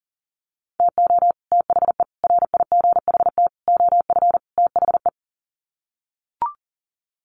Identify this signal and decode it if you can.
Morse code, heard as a keyed tone, about 40 wpm, 700 Hz. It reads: TO THE RIGHT OF THE
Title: Sets of 5 Words Encore 40wpm